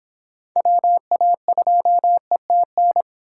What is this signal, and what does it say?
Morse code, heard as a keyed tone, about 26 wpm, 700 Hz. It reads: WA2ETN
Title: Speed Racing - USA 2x3 Call Signs 22wpm